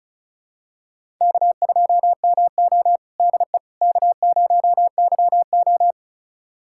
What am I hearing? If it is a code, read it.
Morse code, heard as a keyed tone, about 34 wpm, 700 Hz. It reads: K2MO DE K0YO
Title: USA Callsign DE Callsign - 4 Characters - 35wpm